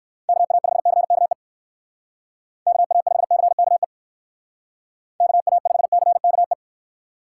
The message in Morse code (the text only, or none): KA5CKE KA5CKE KA5CKE